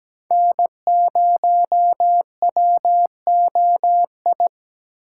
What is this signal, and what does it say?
Morse code, heard as a keyed tone, about 17 wpm, 700 Hz. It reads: N0WOI
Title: K1USN SST QSO Practice - 17wpm